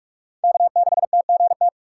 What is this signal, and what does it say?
Morse code, heard as a keyed tone, about 46 wpm, 700 Hz. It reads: K6TGT
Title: USA Call Signs - 5 Characters - 2x Character Spacing - 45wpm